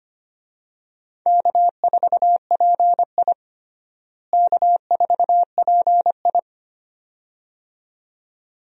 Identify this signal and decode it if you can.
Morse code, heard as a keyed tone, about 25 wpm, 700 Hz. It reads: K4PI K4PI
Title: USA Call Signs - 4 Characters - Repeated 2x - 25wpm